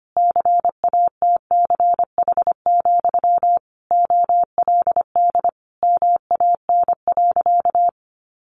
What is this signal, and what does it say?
Morse code, heard as a keyed tone, about 25 wpm, 700 Hz. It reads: CATCH, OLD MAN.
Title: Sentences from Top 1500 Words 25wpm